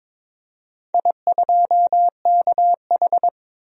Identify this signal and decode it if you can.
Morse code, heard as a keyed tone, about 22 wpm, 690 Hz. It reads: I2KH
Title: DXCC-100 Call Signs - 4 Characters - 22wpm